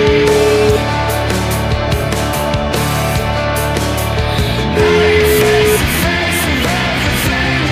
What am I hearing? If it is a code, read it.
Ringback tone (Europe, 425 Hz), with noise.